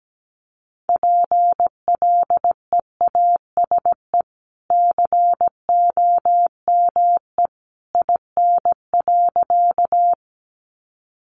Morse code, heard as a keyed tone, about 17 wpm, 700 Hz. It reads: PLEASE COME IN.
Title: Sentences from Top 1500 Words 17wpm